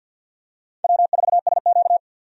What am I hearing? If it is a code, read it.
Morse code, heard as a keyed tone, about 48 wpm, 700 Hz. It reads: W4SQ
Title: USA Call Signs - 4 Characters - 50wpm